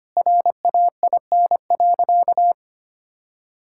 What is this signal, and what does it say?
Morse code, heard as a keyed tone, about 25 wpm, 700 Hz. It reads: RAIN.
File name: practice / 25wpm / All-Age Appropriate Jokes - 25wpm